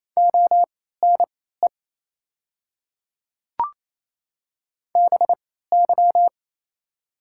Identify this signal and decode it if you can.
Morse code, heard as a keyed tone, about 27 wpm, spaced as for 16 wpm, 690 Hz. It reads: ONE BY
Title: 3x Character Spacing - Top 100 Words - 28wpm